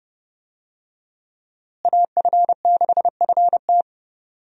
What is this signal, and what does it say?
Morse code, heard as a keyed tone, about 30 wpm, 700 Hz. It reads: AF6FT
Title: USA Call Signs - 5 Characters - Repeated 3x - 30wpm